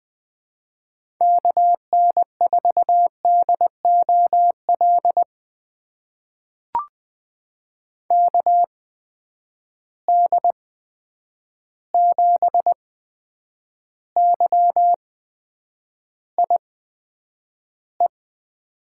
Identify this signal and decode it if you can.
Morse code, heard as a keyed tone, about 20 wpm, spaced as for 9 wpm, 700 Hz. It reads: KN4DOL KD7YIE
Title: USA Call Signs - 8x Character Spacing - 20wpm